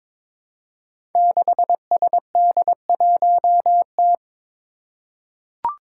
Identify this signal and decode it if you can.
Morse code, heard as a keyed tone, about 22 wpm, 700 Hz. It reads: 6SD1T
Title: Five Challenging Letter-Number Combinations 22wpm